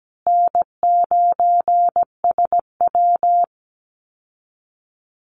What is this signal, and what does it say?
Morse code, heard as a keyed tone, about 17 wpm, 700 Hz. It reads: N9SW
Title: Easy Call Signs Repeated 2x - 17wpm